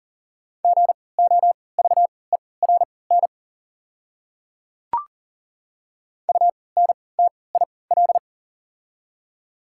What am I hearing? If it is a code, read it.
Morse code, heard as a keyed tone, about 37 wpm, spaced as for 22 wpm, 690 Hz. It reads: GOVERN UNTIL
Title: 3x Character Spacing - Top 500 Words - 40wpm